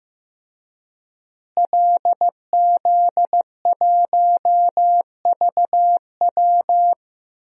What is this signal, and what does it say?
Morse code, heard as a keyed tone, about 15 wpm, 700 Hz. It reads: LZ1VW
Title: DXCC-100 Call Signs - 5 Characters - Repeated 3x - 15wpm